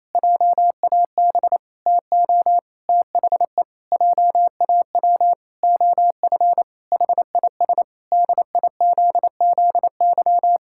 Morse code, heard as a keyed tone, about 28 wpm, 700 Hz. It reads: JAB TO THE JAW OF HIS DIZZY
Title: Pangram Challenge v2 - 28wpm